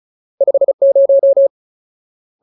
Morse code, heard as a keyed tone, about 35 wpm, 550 Hz. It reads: 50
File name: Ultimate Word List / 35WPM Ultimate